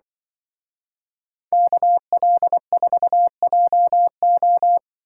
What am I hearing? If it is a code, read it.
Morse code, heard as a keyed tone, about 25 wpm, 700 Hz. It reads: KL4JO
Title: Speed Racing - USA 2x2 Call Signs 20wpm